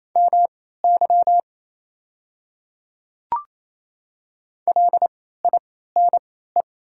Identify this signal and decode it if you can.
Morse code, heard as a keyed tone, about 28 wpm, spaced as for 16 wpm, 690 Hz. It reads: MY LINE
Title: 3x Character Spacing - Top 200 Words - 28wpm